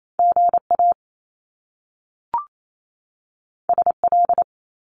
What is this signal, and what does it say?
Morse code, heard as a keyed tone, about 28 wpm, 700 Hz. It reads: GA SL
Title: Top Two Letter Combinations in Words 28wpm